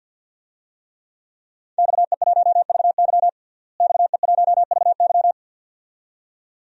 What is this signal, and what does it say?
Morse code, heard as a keyed tone, about 52 wpm, 700 Hz. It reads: XE1VY XE1VY
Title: DXCC-100 Call Signs - 5 Characters - Repeated 2x - 50wpm